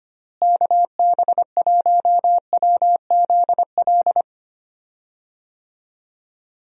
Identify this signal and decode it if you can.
Morse code, heard as a keyed tone, about 25 wpm, 700 Hz. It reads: KB1WZL
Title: USA Call Signs - 6 Characters - Repeated 2x - 25wpm